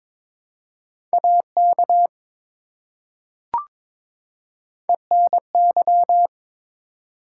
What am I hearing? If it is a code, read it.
Morse code, heard as a keyed tone, about 22 wpm, 700 Hz. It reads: AK ENY
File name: practice / 22wpm / ARRL Sections v2 - 22wpm